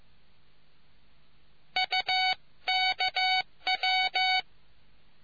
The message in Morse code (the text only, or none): UKW